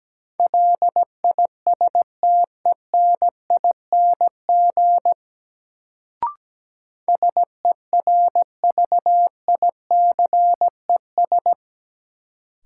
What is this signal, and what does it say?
Morse code, heard as a keyed tone, about 17 wpm, 700 Hz. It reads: LISTENING SERVICES
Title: Sets of 5 Words - New Word Drill - 17wpm